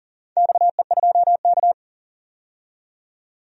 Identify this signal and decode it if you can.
Morse code, heard as a keyed tone, about 40 wpm, 700 Hz. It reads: XE2K
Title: DXCC-100 Call Signs - 4 Characters - 40wpm